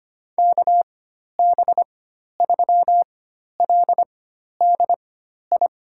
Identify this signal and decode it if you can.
Morse code, heard as a keyed tone, about 25 wpm, spaced as for 12 wpm, 700 Hz. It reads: KB3LDI
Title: USA Call Signs - 4x Character Spacing - 25wpm